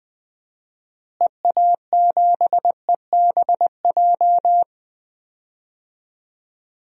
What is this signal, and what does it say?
Morse code, heard as a keyed tone, about 20 wpm, 700 Hz. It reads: EA7EBJ
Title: DXCC-100 Call Signs - 6 Characters - Repeated 2x - 20wpm